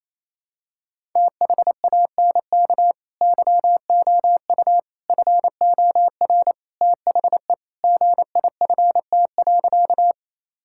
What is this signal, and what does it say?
Morse code, heard as a keyed tone, about 27 wpm, 700 Hz. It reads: THANK YOU FOR THE GIFT.
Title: Sentences from Top 1000 Words - Review Part 6 - 28wpm